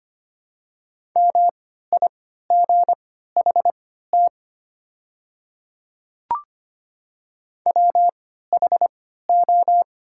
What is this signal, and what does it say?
Morse code, heard as a keyed tone, about 25 wpm, spaced as for 14 wpm, 700 Hz. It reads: MIGHT WHO
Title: 3x Character Spacing - Top 300 Words - 25wpm